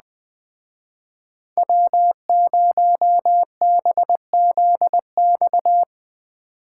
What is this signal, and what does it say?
Morse code, heard as a keyed tone, about 20 wpm, 700 Hz. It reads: W0BZX